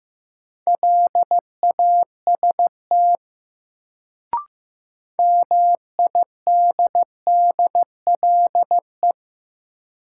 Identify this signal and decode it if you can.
Morse code, heard as a keyed tone, about 15 wpm, 700 Hz. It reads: LAST MIDDLE
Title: Sets of 2 Words - New Word Drill - 15wpm